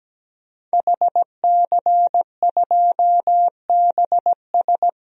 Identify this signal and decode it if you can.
Morse code, heard as a keyed tone, about 17 wpm, 700 Hz. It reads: HC2BS
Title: DXCC-100 Call Signs - 5 Characters - 17wpm